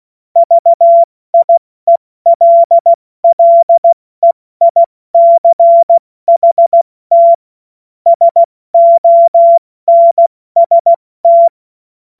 Morse code, heard as a keyed tone, about 16 wpm, 680 Hz. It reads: VIELLEICHT SONST
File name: Morsekurs / Witze / Witz 11. Januar 2022, 70 F80